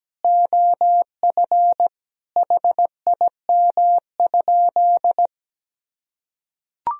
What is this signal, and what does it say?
Morse code, heard as a keyed tone, about 17 wpm, 700 Hz. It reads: OF HIM?
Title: Sentences from Top 100 Words and Abbreviations 17wpm